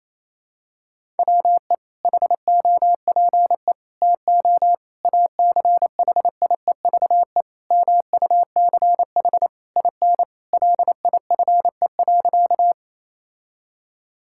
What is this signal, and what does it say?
Morse code, heard as a keyed tone, about 28 wpm, 700 Hz. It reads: WE HOPE TO ACHIEVE MUCH IN LIFE.
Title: Sentences from Top 1000 Words - Review Part 6 - 28wpm